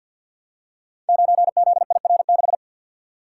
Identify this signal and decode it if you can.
Morse code, heard as a keyed tone, about 48 wpm, 700 Hz. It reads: 9ZIR6
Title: Code Groups - 5 Letter:Numbers - 50wpm